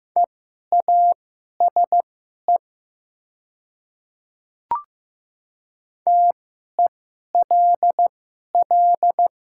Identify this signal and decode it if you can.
Morse code, heard as a keyed tone, about 15 wpm, spaced as for 11 wpm, 700 Hz. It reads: EASE TELL
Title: 2x Character Spacing - Top 300 Words - 15wpm